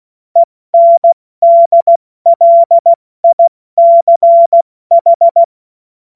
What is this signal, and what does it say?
Morse code, heard as a keyed tone, about 16 wpm, 680 Hz. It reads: ENDLICH
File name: Morsekurs / Witze / Witz 29. Juni 2021, 70 F80